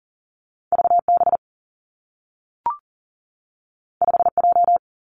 Morse code, heard as a keyed tone, about 42 wpm, 700 Hz. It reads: V B 5 J
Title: Two Challenging Letter-Number Combinations 40wpm